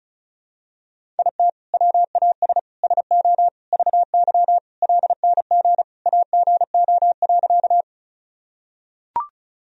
Morse code, heard as a keyed tone, about 36 wpm, 700 Hz. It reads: IT WAS SO VY LNG AGO.
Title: Sentences from Top 500 Words and Abbreviations 35wpm